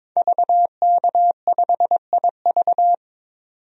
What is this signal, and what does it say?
Morse code, heard as a keyed tone, about 22 wpm, 700 Hz. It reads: VK5IV
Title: DXCC-100 Call Signs - 5 Characters - 22wpm